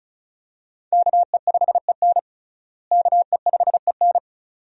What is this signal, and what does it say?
Morse code, heard as a keyed tone, about 36 wpm, 700 Hz. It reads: KE5EN KE5EN
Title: USA Call Signs - 5 Characters - Repeated 2x - 35wpm